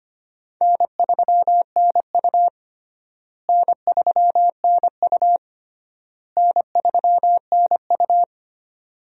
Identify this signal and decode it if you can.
Morse code, heard as a keyed tone, about 25 wpm, 700 Hz. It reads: N3NU N3NU N3NU
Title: USA Call Signs - 4 Characters - Repeated 3x - 25wpm